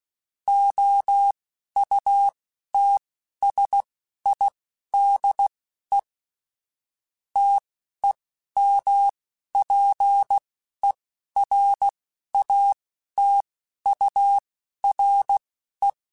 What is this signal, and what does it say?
Morse code, heard as a keyed tone, about 16 wpm, spaced as for 12 wpm, 800 Hz. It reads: OUTSIDE TEMPERATURE
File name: Spaced Characters / Spaced Characters-mp3s / Weather 5 12 sp @15